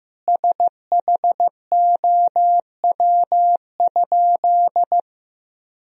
Morse code, heard as a keyed tone, about 15 wpm, 700 Hz. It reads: SHOW?
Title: Sentences from Top 1000 Words - Review Part 2 - 15wpm